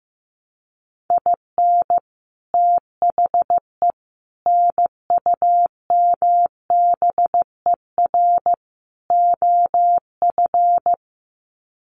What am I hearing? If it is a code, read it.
Morse code, heard as a keyed tone, about 15 wpm, 700 Hz. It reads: IN THE NUMBER OF
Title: Sets of 4 Words 15wpm